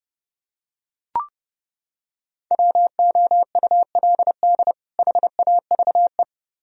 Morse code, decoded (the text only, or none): WOULD HAVE